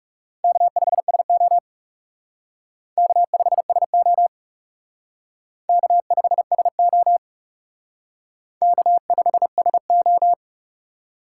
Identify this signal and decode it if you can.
Morse code, heard as a keyed tone, about 36 wpm, 700 Hz. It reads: K5SO K5SO K5SO K5SO
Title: Speed Racing - USA 1x2 Call Signs 30wpm